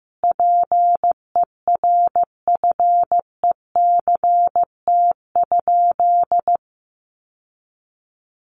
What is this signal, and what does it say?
Morse code, heard as a keyed tone, about 15 wpm, 700 Hz. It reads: PERFECT?
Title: Sentences from Top 1000 Words 15wpm